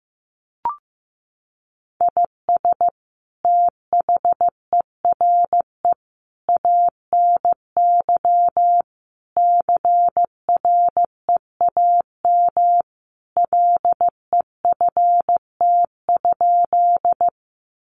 Morse code, heard as a keyed tone, about 15 wpm, 700 Hz. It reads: IS THERE ANY CREAM LEFT?
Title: Sentences from Top 1500 Words 15wpm